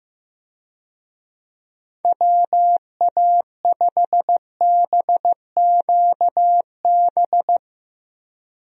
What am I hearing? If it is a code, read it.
Morse code, heard as a keyed tone, about 15 wpm, 700 Hz. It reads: WA5BQB
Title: USA Call Signs - 6 Characters - 15wpm